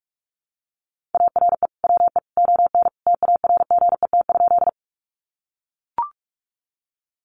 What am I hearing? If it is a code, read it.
Morse code, heard as a keyed tone, about 44 wpm, 700 Hz. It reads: ARE WE ON TARGET?
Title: Sentences from Top 1200 Words 45wpm